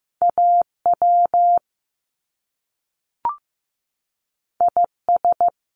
Morse code, heard as a keyed tone, about 15 wpm, 700 Hz. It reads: AW IS
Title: Top Two Letter Combinations in Words 15wpm